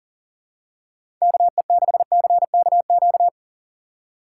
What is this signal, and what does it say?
Morse code, heard as a keyed tone, about 40 wpm, 700 Hz. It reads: KE6CKQ
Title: USA Call Signs - 6 Characters - 3x Character Spacing - 40wpm